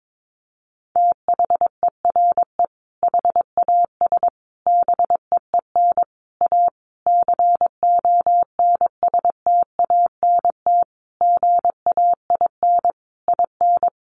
Morse code, heard as a keyed tone, about 22 wpm, 700 Hz. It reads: THERE HAS BEEN A CONSTANT GAIN IN
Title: Sentences from Top 1500 Words 22wpm